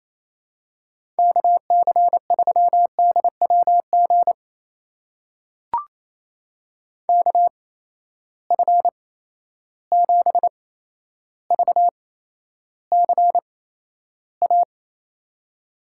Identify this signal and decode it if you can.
Morse code, heard as a keyed tone, about 28 wpm, spaced as for 12 wpm, 700 Hz. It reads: KC3DWG KF7VCA
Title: USA Call Signs - 8x Character Spacing - 28wpm